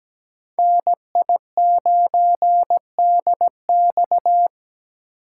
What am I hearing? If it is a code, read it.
Morse code, heard as a keyed tone, about 17 wpm, 700 Hz. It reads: NI9DX
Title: USA Call Signs - 5 Characters - 2x Character Spacing - 17wpm